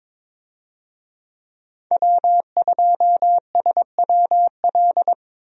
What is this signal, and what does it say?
Morse code, heard as a keyed tone, about 22 wpm, 700 Hz. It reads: W2SWL
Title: USA Call Signs - 5 Characters - Repeated 2x - 22wpm